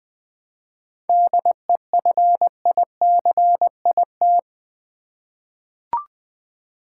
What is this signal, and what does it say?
Morse code, heard as a keyed tone, about 20 wpm, 700 Hz. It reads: DEFICIT
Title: 7 Letter Words - First 3 Letters Given - 20wpm